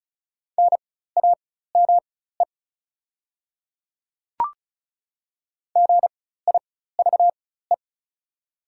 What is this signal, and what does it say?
Morse code, heard as a keyed tone, about 36 wpm, spaced as for 16 wpm, 700 Hz. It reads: NAME GIVE
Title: 4x Character Spacing - Top 200 Words - 35wpm